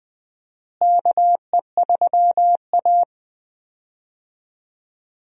Morse code, heard as a keyed tone, about 20 wpm, 700 Hz. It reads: KE3A
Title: USA Call Signs - 4 Characters - Repeated 2x - 20wpm